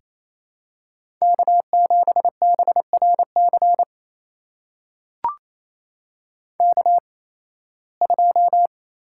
Morse code, heard as a keyed tone, about 27 wpm, spaced as for 18 wpm, 700 Hz. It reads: K7BRC K2
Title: USA Call Signs - 5 Characters - 8x Character Spacing - 28wpm